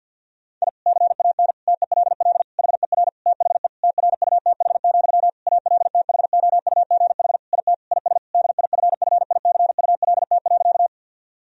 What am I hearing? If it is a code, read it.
Morse code, heard as a keyed tone, about 51 wpm, 700 Hz. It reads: I CAN TELL HER THE TRUTH, ALTHOUGH IT IS DIFFICULT.
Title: Sentences from Top 1000 Words - Review Part 3 - 50wpm